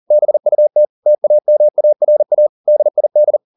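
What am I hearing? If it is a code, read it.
Morse code, heard as a keyed tone, about 40 wpm, 600 Hz. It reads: BUT TAMARA DID
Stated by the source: Morse code at 40 wpm, practice text, CW med Gnister.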